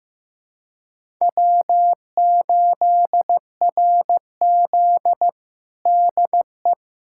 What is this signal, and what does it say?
Morse code, heard as a keyed tone, about 15 wpm, 700 Hz. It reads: W8RZ DE
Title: USA Callsign DE Callsign - 4 Characters - 15wpm